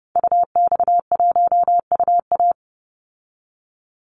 Morse code, heard as a keyed tone, about 30 wpm, 700 Hz. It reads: UX1UA
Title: Call Signs - 30wpm